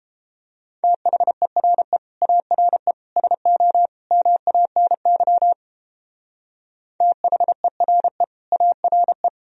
Morse code, heard as a keyed tone, about 32 wpm, 700 Hz. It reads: THERE ARE SO MANY THERE ARE